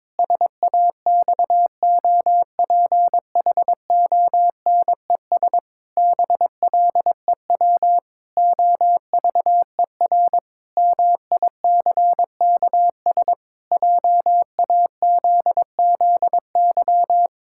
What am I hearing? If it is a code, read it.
Morse code, heard as a keyed tone, about 22 wpm, 700 Hz. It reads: SAXOPHONES BLEW OVER MICKS JAZZY
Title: Pangram Challenge v2 - 22wpm